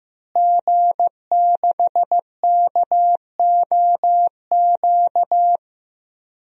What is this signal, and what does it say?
Morse code, heard as a keyed tone, about 15 wpm, 700 Hz. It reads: G6KOQ
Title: DXCC-100 Call Signs - 5 Characters - Repeated 2x - 15wpm